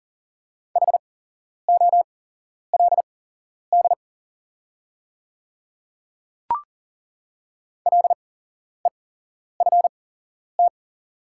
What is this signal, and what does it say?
Morse code, heard as a keyed tone, about 40 wpm, spaced as for 11 wpm, 690 Hz. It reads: HOLD LEFT
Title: 8x Character Spacing - Top 500 Words - 40wpm